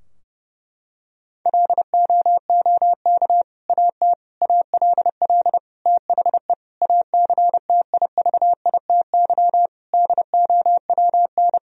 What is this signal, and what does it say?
Morse code, heard as a keyed tone, about 30 wpm, 700 Hz. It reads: LOOK AT ALL THE ACTIVITY DOWN
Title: Sentences from Top 1000 Words - Review Part 4 - 30wpm